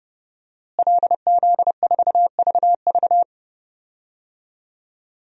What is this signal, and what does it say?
Morse code, heard as a keyed tone, about 30 wpm, 700 Hz. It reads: LZ4VV
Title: DXCC-100 Call Signs - 5 Characters - Repeated 2x - 30wpm